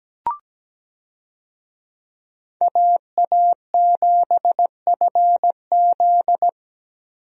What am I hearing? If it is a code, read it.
Morse code, heard as a keyed tone, about 17 wpm, 700 Hz. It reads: AA7FZ
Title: USA Call Signs - 5 Characters - Repeated 2x - 17wpm